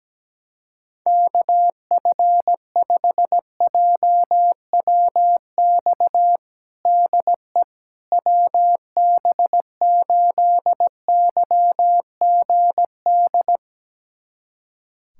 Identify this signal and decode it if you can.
Morse code, heard as a keyed tone, about 17 wpm, 700 Hz. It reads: KF5JWX DE WB8YGD
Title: USA Callsign DE Callsign 17wpm